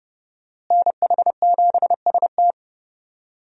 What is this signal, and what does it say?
Morse code, heard as a keyed tone, about 30 wpm, 700 Hz. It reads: NH7ST
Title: USA Call Signs - 5 Characters - Repeated 3x - 30wpm